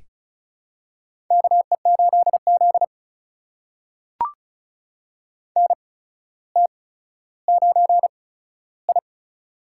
Morse code, heard as a keyed tone, about 36 wpm, spaced as for 15 wpm, 700 Hz. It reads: KE8Z NT9I
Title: USA Call Signs - 4 Characters - 8x Character Spacing - 35wpm